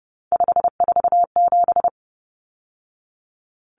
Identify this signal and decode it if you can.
Morse code, heard as a keyed tone, about 30 wpm, 700 Hz. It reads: 547
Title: Numbers - 3-Digit 30wpm